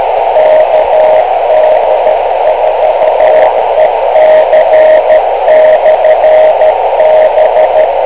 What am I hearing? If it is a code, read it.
Morse code, heard as a keyed tone, about 13 wpm, 620 Hz, with radio noise. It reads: KD4EC/B
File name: kl7fdq / beacons / kd4ec-rst429